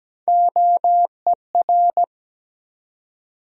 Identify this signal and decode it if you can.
Morse code, heard as a keyed tone, about 17 wpm, 700 Hz. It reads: OER
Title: USA Call Sign Postfixes - 3 Letters 17wpm